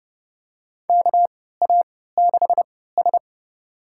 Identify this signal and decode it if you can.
Morse code, heard as a keyed tone, about 30 wpm, spaced as for 17 wpm, 700 Hz. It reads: KA6S
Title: USA Call Signs - 4 Characters - 3x Character Spacing - 30wpm